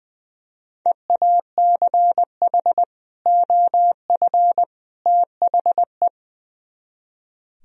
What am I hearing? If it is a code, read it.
Morse code, heard as a keyed tone, about 20 wpm, 700 Hz. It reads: EACH OF THE
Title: Speed-Racing - Sets of 3 Words 20wpm